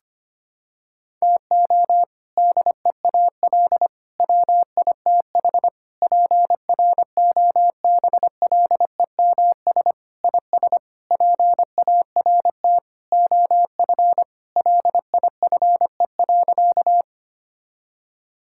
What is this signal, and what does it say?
Morse code, heard as a keyed tone, about 25 wpm, 700 Hz. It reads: TO DEAL WITH PROBLEMS IS PART OF LIFE.
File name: practice / 25wpm / Sentences with 3-Word Phrases - 25wpm